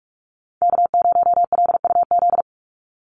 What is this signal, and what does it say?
Morse code, heard as a keyed tone, about 44 wpm, 700 Hz. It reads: K0LUZ
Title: Call Signs Repeated 2x - 45wpm